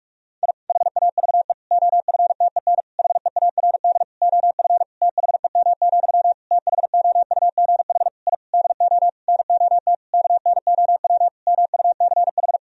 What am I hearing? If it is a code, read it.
Morse code, heard as a keyed tone, about 41 wpm, 700 Hz. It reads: I HAVE OFTEN HEARD OF THEM, THOUGH I DO NOT KNOW MUCH
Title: Sentences from Top 1000 Words - Review Part 2 - 45wpm